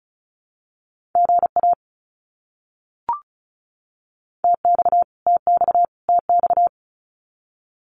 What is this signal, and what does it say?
Morse code, heard as a keyed tone, about 34 wpm, 700 Hz. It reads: GA TX TX TX
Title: Abbreviations Repeated 3x - 35wpm